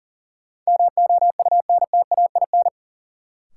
Morse code, heard as a keyed tone, about 38 wpm, 700 Hz. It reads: MOUNTAIN
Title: Sentences from Top 1000 Words - Review Part 1 - New Word Drill - 40wpm